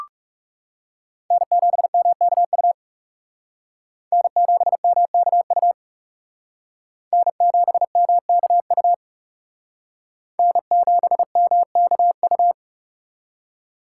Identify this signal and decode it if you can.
Morse code, heard as a keyed tone, about 38 wpm, 700 Hz. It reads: N7MKU N7MKU N7MKU N7MKU